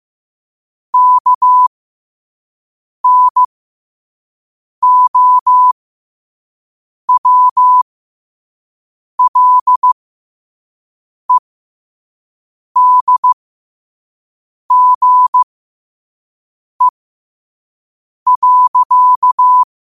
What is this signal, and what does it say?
Morse code, heard as a keyed tone, about 15 wpm, spaced as for 5 wpm, 1000 Hz. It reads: KNOWLEDGE.
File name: qotdfiles / 5wpm